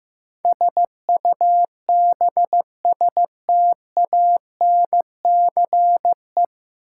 Morse code, heard as a keyed tone, about 15 wpm, 700 Hz. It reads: SUBSTANCE
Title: Sentences from Top 1500 Words - New Word Drill Alt - 15wpm